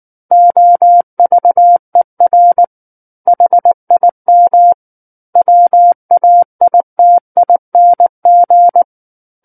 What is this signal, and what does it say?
Morse code, heard as a keyed tone, about 19 wpm, 700 Hz. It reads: OVER HIM WAITING
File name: Book One / 10-Chapter 10 - 19 WPM